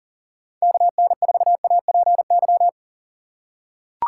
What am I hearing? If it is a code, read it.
Morse code, heard as a keyed tone, about 40 wpm, 700 Hz. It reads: KN4APY
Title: USA Call Signs - 6 Characters - 3x Character Spacing - 40wpm